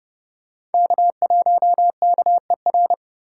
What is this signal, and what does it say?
Morse code, heard as a keyed tone, about 32 wpm, 700 Hz. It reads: K1KER
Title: USA Call Signs - 2x Character Spacing - 30wpm